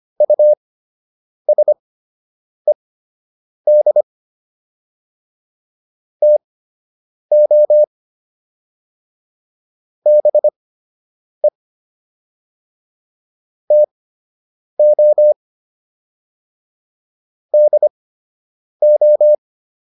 Morse code, heard as a keyed tone, about 25 wpm, spaced as for 8 wpm, 600 Hz. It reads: USED TO BE TO DO